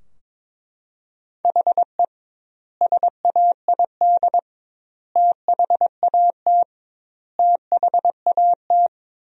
Morse code, heard as a keyed tone, about 22 wpm, 700 Hz. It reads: HE SAID THAT THAT
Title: 2x Word-Spacing - Sentences from Top 200 Words - 22wpm